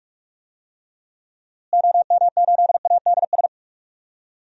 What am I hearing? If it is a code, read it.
Morse code, heard as a keyed tone, about 46 wpm, 700 Hz. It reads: OM8ADS
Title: DXCC-100 Call Signs - 6 Characters - Repeated 3x - 45wpm